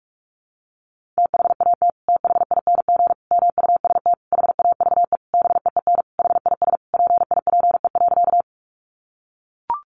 Morse code, heard as a keyed tone, about 44 wpm, 700 Hz. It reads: THAT THING MUST HAVE BEEN HIS PIPE.
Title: Sentences from Top 1500 Words 45wpm